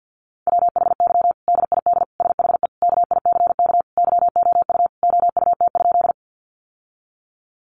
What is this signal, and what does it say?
Morse code, heard as a keyed tone, about 50 wpm, 700 Hz. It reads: WHY DID SHE KICK YOU OUT?
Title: Sentences from Top 1300 Words 50wpm